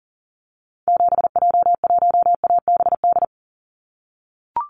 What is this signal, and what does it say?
Morse code, heard as a keyed tone, about 42 wpm, 700 Hz. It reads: 7J1ABD
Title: Call Signs - 40wpm